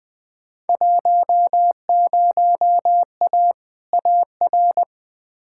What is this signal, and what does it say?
Morse code, heard as a keyed tone, about 20 wpm, 700 Hz. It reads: 10A AR
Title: Field Day Exchanges Encore - 20wpm